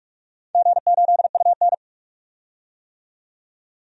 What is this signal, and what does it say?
Morse code, heard as a keyed tone, about 46 wpm, 700 Hz. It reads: G8UN
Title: DXCC-100 Call Signs - 4 Characters - 45wpm